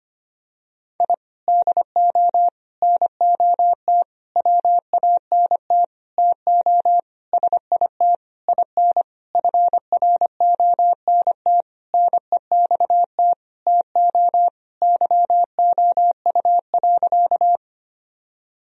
Morse code, heard as a keyed tone, about 25 wpm, 700 Hz. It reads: I DO NOT WANT TO SIT IN FRONT NEXT TO YOU.
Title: Sentences from Top 1000 Words - Review Part 2 - 25wpm